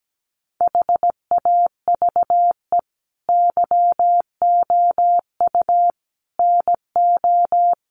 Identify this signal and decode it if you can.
Morse code, heard as a keyed tone, about 17 wpm, 700 Hz. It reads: HAVE YOU NO
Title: Sentences from Top 1500 Words 17wpm